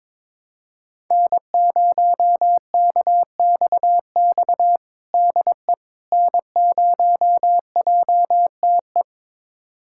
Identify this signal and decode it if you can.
Morse code, heard as a keyed tone, about 22 wpm, 700 Hz. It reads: N0KXX DE N0JTE
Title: USA Callsign DE Callsign - 5 Characters - 22wpm